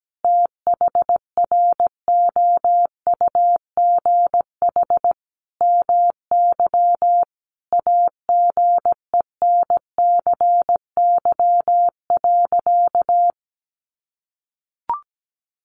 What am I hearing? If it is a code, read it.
Morse code, heard as a keyed tone, about 17 wpm, 700 Hz. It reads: THROUGH MY AGENCY.
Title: Sentences from Top 1300 Words 17wpm